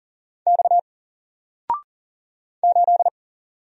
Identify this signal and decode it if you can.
Morse code, heard as a keyed tone, about 40 wpm, 700 Hz. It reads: X 8
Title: Single Character (Letter:Number:Punctuation:Prosign) Alt 40wpm